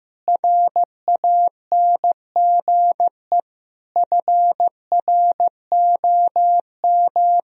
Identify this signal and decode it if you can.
Morse code, heard as a keyed tone, about 15 wpm, 700 Hz. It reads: RANGE FROM
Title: Sentences from Top 1000 Words - Review Part 4 - 15wpm